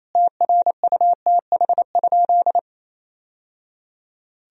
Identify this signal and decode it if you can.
Morse code, heard as a keyed tone, about 28 wpm, 700 Hz. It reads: TRUTH?
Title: Sentences from Top 1000 Words - Review Part 3 - 28wpm